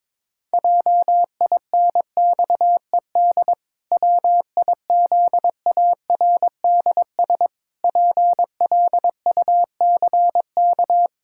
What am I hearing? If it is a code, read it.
Morse code, heard as a keyed tone, about 22 wpm, 700 Hz. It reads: JINXED WIZARDS PLUCK
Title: Pangram Challenge v2 - 22wpm